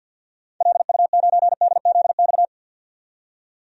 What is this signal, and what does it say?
Morse code, heard as a keyed tone, about 52 wpm, 700 Hz. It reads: LU9DZX
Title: DXCC-100 Call Signs - 6 Characters - 50wpm